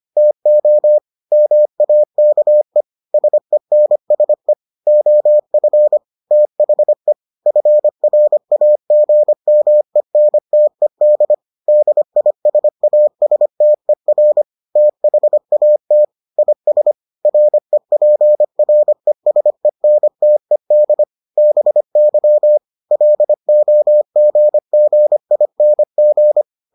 Morse code, heard as a keyed tone, about 25 wpm, 600 Hz. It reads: TO MAKE SENSE OF THE FRAGMENTED DISASTER THAT IS REPRESENTED BY LOGGING